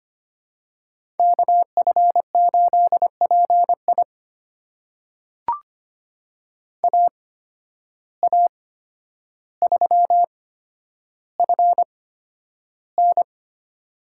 Morse code, heard as a keyed tone, about 25 wpm, spaced as for 11 wpm, 700 Hz. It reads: KF8PI AA3FN
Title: USA Call Signs - 5 Characters - 8x Character Spacing - 25wpm